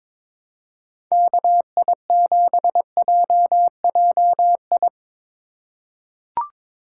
Morse code, heard as a keyed tone, about 21 wpm, 700 Hz. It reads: KI7JJI
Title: USA Call Signs - 6 Characters - 4x Character Spacing - 22wpm